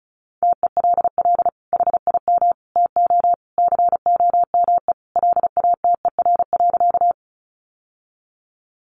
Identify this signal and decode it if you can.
Morse code, heard as a keyed tone, about 34 wpm, 700 Hz. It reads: TELL HIM TO COME LATER.
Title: Sentences from Top 700 Words 35wpm